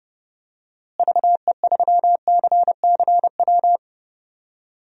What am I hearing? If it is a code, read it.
Morse code, heard as a keyed tone, about 29 wpm, 700 Hz. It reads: VE3CCW
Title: DXCC-100 Call Signs - 6 Characters - Repeated 3x - 30wpm